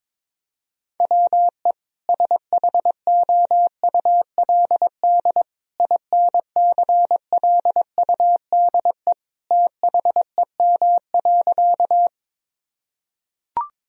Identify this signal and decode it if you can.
Morse code, heard as a keyed tone, about 22 wpm, 700 Hz. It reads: WE SHOULD INCLUDE THEM.
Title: Sentences from Top 1000 Words - Review Part 3 - 22wpm